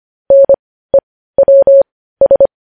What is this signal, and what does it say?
Morse code, heard as a keyed tone, about 25 wpm, spaced as for 15 wpm, 550 Hz. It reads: NEWS
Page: News Headlines in Morse Code at 15 WPM